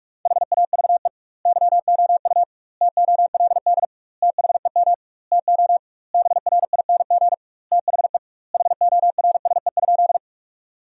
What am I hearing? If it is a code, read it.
Morse code, heard as a keyed tone, about 45 wpm, 700 Hz. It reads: HAVE YOU TOLD THEM TO BRING THE HORSE?
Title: 2x Word-Spacing - Sentences from Top 500 Words - 45wpm